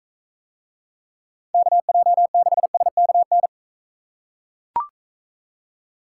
Morse code, heard as a keyed tone, about 41 wpm, 700 Hz. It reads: KJ6SKN